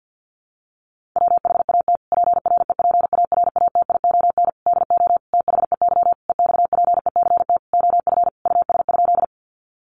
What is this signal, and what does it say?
Morse code, heard as a keyed tone, about 50 wpm, 700 Hz. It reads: WHAT PREPARATION DO THEY EXPECT OF US?
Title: Sentences from Top 1400 Words 50wpm